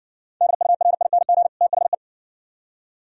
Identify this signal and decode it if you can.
Morse code, heard as a keyed tone, about 60 wpm, 700 Hz. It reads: DURING THE